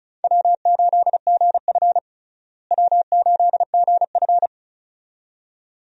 Morse code, heard as a keyed tone, about 36 wpm, 700 Hz. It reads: W8GF W8GF
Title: USA Call Signs - 4 Characters - Repeated 2x - 35wpm